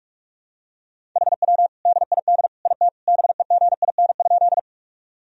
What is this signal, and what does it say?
Morse code, heard as a keyed tone, about 46 wpm, 700 Hz. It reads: HW DID IT BEGIN?
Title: Sentences from Top 300 Words and Abbreviations 45wpm